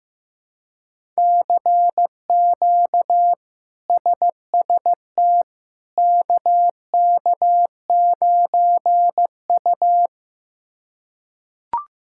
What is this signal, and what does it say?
Morse code, heard as a keyed tone, about 15 wpm, 700 Hz. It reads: CQ SST KK9U
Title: K1USN SST QSO Practice - 15wpm